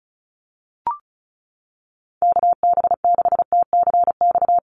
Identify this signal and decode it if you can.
Morse code, heard as a keyed tone, about 34 wpm, 700 Hz. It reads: KB6TCX
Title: USA Call Signs Repeated 2x 35wpm